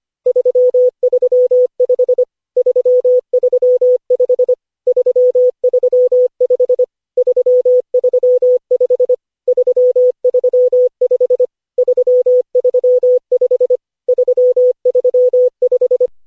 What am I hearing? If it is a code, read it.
Morse code, heard as a keyed tone, about 25 wpm, 500 Hz. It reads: 335 335 335 335 335 335 335
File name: cw / Session5 cw training